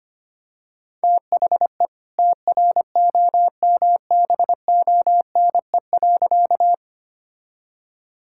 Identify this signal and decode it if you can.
Morse code, heard as a keyed tone, about 25 wpm, 700 Hz. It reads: THE TROMBONE.
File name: practice / 25wpm / All-Age Appropriate Jokes - 25wpm